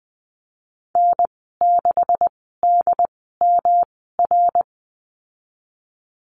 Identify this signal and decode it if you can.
Morse code, heard as a keyed tone, about 19 wpm, spaced as for 14 wpm, 700 Hz. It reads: N6DMR
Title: 154 Lesson 25g - Letter p - Call Signs 20wpm